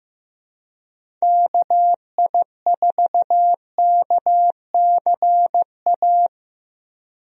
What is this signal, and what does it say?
Morse code, heard as a keyed tone, about 15 wpm, 700 Hz. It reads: KI4KCA
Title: USA Call Signs - 6 Characters - Repeated 3x - 15wpm